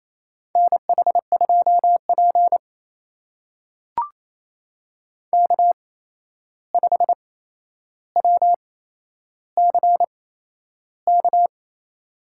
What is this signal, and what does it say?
Morse code, heard as a keyed tone, about 28 wpm, spaced as for 11 wpm, 700 Hz. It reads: NH2P K5WCK
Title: USA Call Signs - 8x Character Spacing - 28wpm